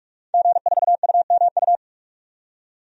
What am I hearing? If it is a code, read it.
Morse code, heard as a keyed tone, about 42 wpm, 700 Hz. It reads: G4UMU
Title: DXCC-100 Call Signs - 5 Characters - 45wpm